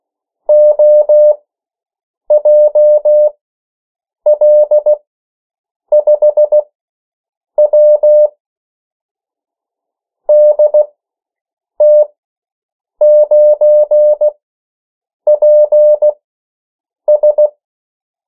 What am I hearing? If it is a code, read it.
Morse code, heard as a keyed tone, about 16 wpm, spaced as for 7 wpm, 600 Hz. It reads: OJL5W DT9PS
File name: morse mp3 / 14 urok 13